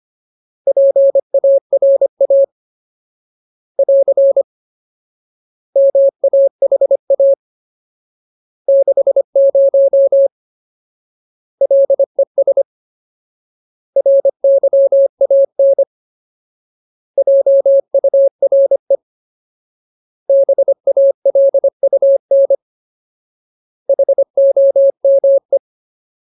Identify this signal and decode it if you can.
Morse code, heard as a keyed tone, about 25 wpm, 550 Hz. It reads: PARA <AR> MAHA 60 LES RYAN JURE BALUN HOME
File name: Ultimate Word List / 25WPM Ultimate